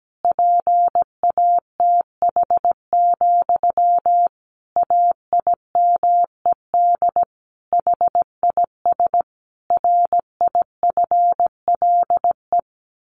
Morse code, heard as a keyed tone, about 17 wpm, 700 Hz. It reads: PATH, AIMED HIS RIFLE
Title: A Sound Of Thunder 17wpm